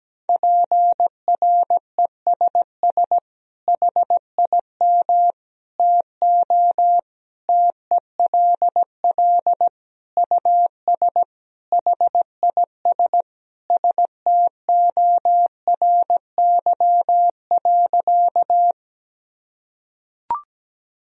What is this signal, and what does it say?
Morse code, heard as a keyed tone, about 17 wpm, 700 Hz. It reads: PRESS HIM TO TELL US HIS STORY.
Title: Sentences from Top 1000 Words - Review Part 1 - 17wpm